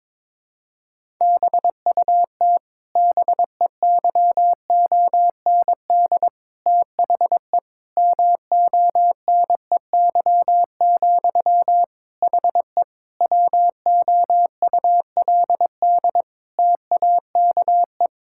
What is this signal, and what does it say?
Morse code, heard as a keyed tone, about 22 wpm, 700 Hz. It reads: BUT BEYOND THE MONEY, HE WOULD TAKE
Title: Sentences from Top 1000 Words - Review Part 6 - 22wpm